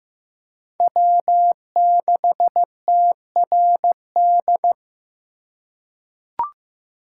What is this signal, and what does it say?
Morse code, heard as a keyed tone, about 15 wpm, 700 Hz. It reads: W6TRD
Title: USA Call Signs - 5 Characters - 4x Character Spacing - 15wpm